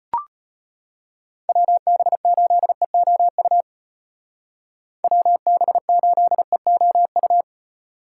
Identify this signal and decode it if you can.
Morse code, heard as a keyed tone, about 33 wpm, 700 Hz. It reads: WB8EOU WB8EOU